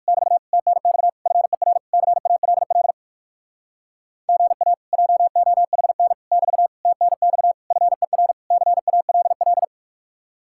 Morse code, heard as a keyed tone, about 65 wpm, spaced as for 56 wpm, 700 Hz. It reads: <BT> TNX FER CALL GA JOHN <BT> TNX FER CALL
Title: Speed-Racing Common QSO Phrases - 45wpm